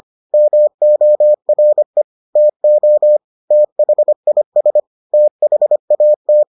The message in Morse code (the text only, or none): MORE TO THIS THAT